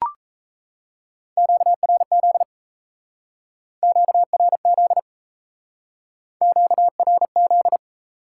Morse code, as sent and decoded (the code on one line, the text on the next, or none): --.- .-. --.. / --.- .-. --.. / --.- .-. --..
QRZ QRZ QRZ